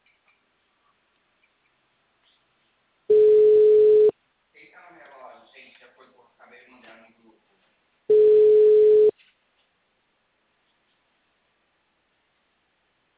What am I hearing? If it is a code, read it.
Ringback tone (Europe, 425 Hz), with speech.
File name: out- -2039- - - .8